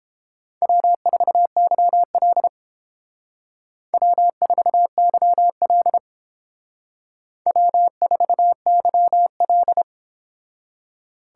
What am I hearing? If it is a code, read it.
Morse code, heard as a keyed tone, about 29 wpm, 700 Hz. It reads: W4YL W4YL W4YL